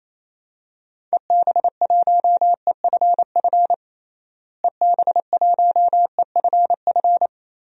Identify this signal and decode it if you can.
Morse code, heard as a keyed tone, about 27 wpm, 700 Hz. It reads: EB1EFF EB1EFF
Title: DXCC-100 Call Signs - 6 Characters - Repeated 3x - 28wpm